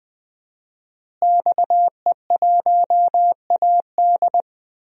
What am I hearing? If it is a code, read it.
Morse code, heard as a keyed tone, about 20 wpm, 700 Hz. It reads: XE1AD
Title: DXCC-100 Call Signs - 5 Characters - Repeated 2x - 20wpm